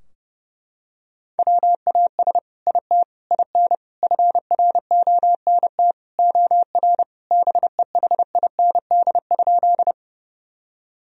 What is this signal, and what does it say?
Morse code, heard as a keyed tone, about 30 wpm, 700 Hz. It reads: WAS IT IN FRONT OR BEHIND?
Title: Sentences from Top 1000 Words - Review Part 2 - 30wpm